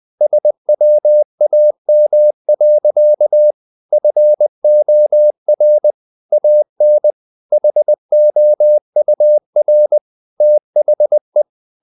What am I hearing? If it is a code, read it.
Morse code, heard as a keyed tone, about 20 wpm, 600 Hz. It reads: SWAM. FOR AN HOUR THE
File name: the old man and the sea part 2 20wpm